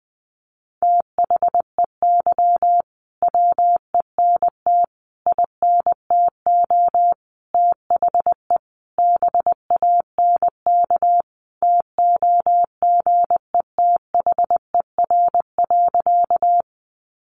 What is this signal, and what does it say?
Morse code, heard as a keyed tone, about 20 wpm, 700 Hz. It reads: THEY WENT INTO THE BANK TOGETHER.
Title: Sentences from Top 900 Words 20wpm